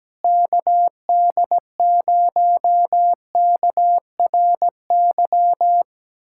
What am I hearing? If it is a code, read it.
Morse code, heard as a keyed tone, about 17 wpm, 700 Hz. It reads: KD0KRY
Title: USA Callsign DE Callsign - 6 Characters - 17wpm